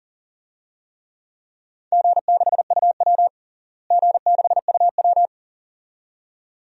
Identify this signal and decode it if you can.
Morse code, heard as a keyed tone, about 37 wpm, 700 Hz. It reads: G6UW G6UW
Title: DXCC-100 Call Signs - 4 Characters - Repeated 2x - 40wpm